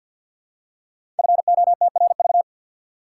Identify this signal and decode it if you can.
Morse code, heard as a keyed tone, about 51 wpm, 700 Hz. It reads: FOTRV
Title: Code Groups - 5 Letters - 50wpm